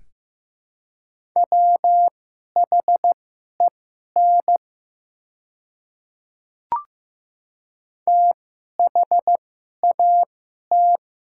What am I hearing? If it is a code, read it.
Morse code, heard as a keyed tone, about 15 wpm, spaced as for 11 wpm, 700 Hz. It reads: WHEN THAT
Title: 2x Character Spacing - Top 300 Words - 15wpm